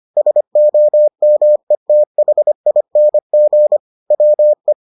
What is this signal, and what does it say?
Morse code, heard as a keyed tone, about 25 wpm, 600 Hz. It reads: SOMETHING WE